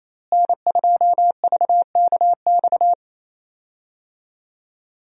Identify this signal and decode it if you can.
Morse code, heard as a keyed tone, about 28 wpm, 700 Hz. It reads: N2VKX
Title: USA Call Signs - 5 Characters - Repeated 2x - 28wpm